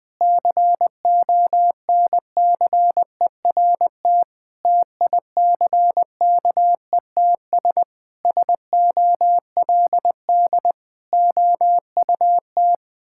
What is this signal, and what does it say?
Morse code, heard as a keyed tone, about 20 wpm, 700 Hz. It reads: CONCERT TICKETS SOLD OUT
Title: Sentences with 6-Word Phrases - 20wpm